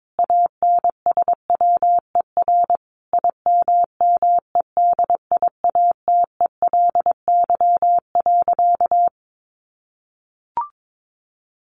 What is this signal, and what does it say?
Morse code, heard as a keyed tone, about 22 wpm, 700 Hz. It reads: ANSWER IMMEDIATELY.
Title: Sentences from Top 1100 Words 22wpm